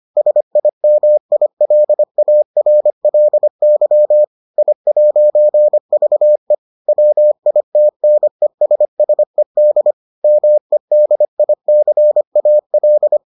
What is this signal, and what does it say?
Morse code, heard as a keyed tone, about 25 wpm, 600 Hz. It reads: SIMILARLY I'VE WITNESSED MEDICAL